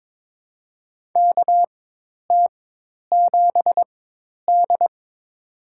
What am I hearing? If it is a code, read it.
Morse code, heard as a keyed tone, about 22 wpm, spaced as for 10 wpm, 700 Hz. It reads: KT7D
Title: USA Call Signs - 4x Character Spacing - 22wpm